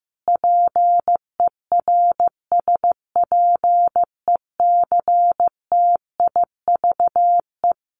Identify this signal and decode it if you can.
Morse code, heard as a keyed tone, about 15 wpm, 700 Hz. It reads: PERSPECTIVE
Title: Sentences from Top 1200 Words 15wpm